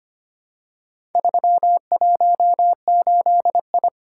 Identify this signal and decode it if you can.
Morse code, heard as a keyed tone, about 25 wpm, 700 Hz. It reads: 318I
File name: practice / 25wpm / Four Letter-Number Combinations - 25wpm